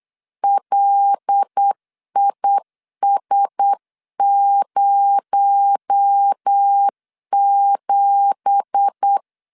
Morse code, heard as a keyed tone, about 8 wpm, 800 Hz. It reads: LIS07
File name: CW / DET / 1 9 1500C S A 8wpm 5637